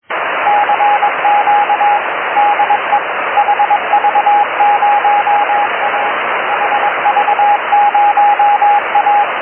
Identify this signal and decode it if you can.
Morse code, heard as a keyed tone, about 22 wpm, 820 Hz, with radio noise. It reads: CQ DE HV0A HV0A
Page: Selection of 70"Mhz signals heard or worked by EI4DQ